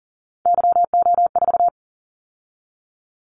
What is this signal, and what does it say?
Morse code, heard as a keyed tone, about 40 wpm, 700 Hz. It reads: YO4
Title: Callsign Prefixes for Top 25 DXCC - 40wpm